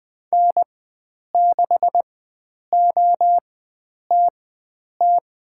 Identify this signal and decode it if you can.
Morse code, heard as a keyed tone, about 20 wpm, spaced as for 9 wpm, 700 Hz. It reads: N6OTT
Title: USA Call Signs - 5 Characters - 4x Character Spacing - 20wpm